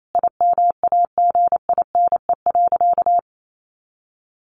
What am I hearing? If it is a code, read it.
Morse code, heard as a keyed tone, about 28 wpm, 700 Hz. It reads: IMAGINE.
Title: Sentences from Top 1400 Words 28wpm